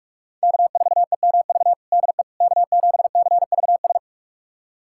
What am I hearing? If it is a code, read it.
Morse code, heard as a keyed tone, about 47 wpm, 700 Hz. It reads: K4EMV DE K7CVS
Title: USA Callsign DE Callsign - 5 Characters - 45wpm